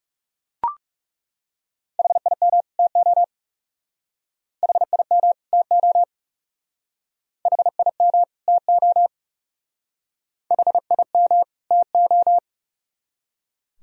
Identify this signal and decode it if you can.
Morse code, heard as a keyed tone, about 36 wpm, 700 Hz. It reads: HIM TO HIM TO HIM TO HIM TO
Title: Speed-Racing - Sets of 2 Words 30wpm